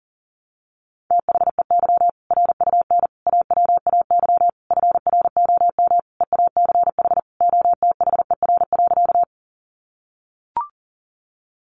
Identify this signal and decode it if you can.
Morse code, heard as a keyed tone, about 40 wpm, 700 Hz. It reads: THEY RUN AWAY FROM EACH OTHER.
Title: Easy Phrases 40wpm